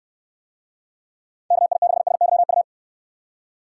Morse code, heard as a keyed tone, about 63 wpm, 700 Hz. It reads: KI6SYU